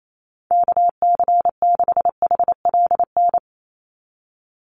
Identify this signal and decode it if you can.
Morse code, heard as a keyed tone, about 28 wpm, 700 Hz. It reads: KC6HLN
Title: USA Call Signs 28wpm